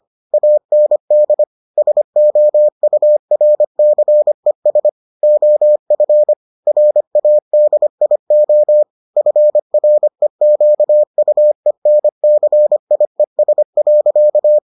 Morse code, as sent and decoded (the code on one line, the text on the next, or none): .- -. -.. / ... --- ..- .-. -.-. ... / --- ..-. / .-. .- -.. .. --- / ..-. .-. --.- ..- . -. -.-. .. ... .-.-.-
AND SOURCES OF RADIO FREQUENCIES.